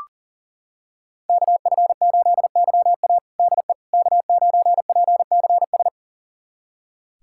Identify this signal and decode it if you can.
Morse code, heard as a keyed tone, about 40 wpm, 700 Hz. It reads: KF8YA DE K9PCS
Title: USA Callsign DE Callsign - 5 Characters - 40wpm